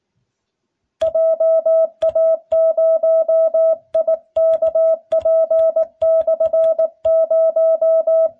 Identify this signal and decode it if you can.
Morse code, heard as a keyed tone, about 19 wpm, 650 Hz. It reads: JA0IKP/0